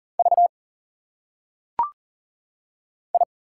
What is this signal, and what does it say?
Morse code, heard as a keyed tone, about 40 wpm, 700 Hz. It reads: V I
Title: Single Letters - Phonetic Pronunciation 40wpm